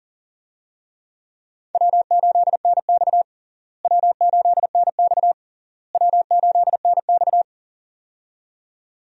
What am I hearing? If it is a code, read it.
Morse code, heard as a keyed tone, about 40 wpm, 700 Hz. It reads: W8NX W8NX W8NX
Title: USA Call Signs - 4 Characters - Repeated 3x - 40wpm